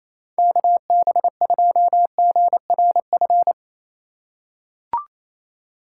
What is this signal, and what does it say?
Morse code, heard as a keyed tone, about 28 wpm, 700 Hz. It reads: KB2GRF
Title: USA Call Signs - 6 Characters - 3x Character Spacing - 28wpm